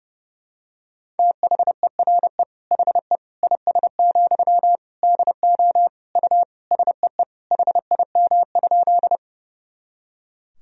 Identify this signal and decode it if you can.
Morse code, heard as a keyed tone, about 30 wpm, 700 Hz. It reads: THERE HE IS, DO U SEE HIM?
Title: Sentences from Top 100 Words and Abbreviations 30wpm